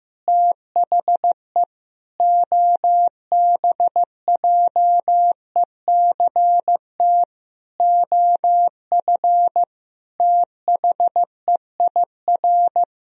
Morse code, heard as a keyed tone, about 15 wpm, 700 Hz. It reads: THE OBJECT OF THEIR
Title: Sentences from Top 1000 Words - Review Part 3 - 15wpm